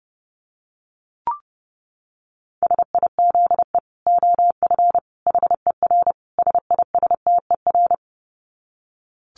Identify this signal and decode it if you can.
Morse code, heard as a keyed tone, about 30 wpm, 700 Hz. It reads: SIZE OF HER SISTER
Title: Dit Challenge 30wpm